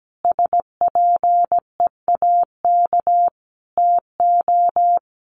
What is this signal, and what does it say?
Morse code, heard as a keyed tone, about 17 wpm, 700 Hz. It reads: SPEAK TO
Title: Sentences from Top 1400 Words 17wpm